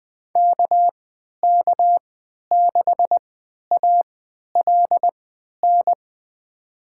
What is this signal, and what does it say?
Morse code, heard as a keyed tone, about 20 wpm, spaced as for 11 wpm, 700 Hz. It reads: KK6ALN
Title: USA Call Signs - 6 Characters - 3x Character Spacing - 20wpm